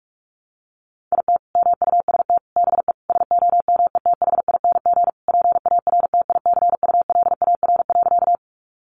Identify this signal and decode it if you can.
Morse code, heard as a keyed tone, about 44 wpm, 700 Hz. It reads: IT MUST BE SOMETHING PARTICULAR.
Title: Sentences from Top 600 Words 45wpm